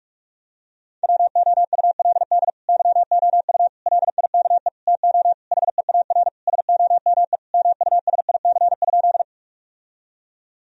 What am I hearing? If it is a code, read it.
Morse code, heard as a keyed tone, about 47 wpm, 700 Hz. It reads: WOULD YOU LIKE TO HEAR SOME MUSIC?
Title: Sentences from Top 1000 Words - Review Part 1 - 45wpm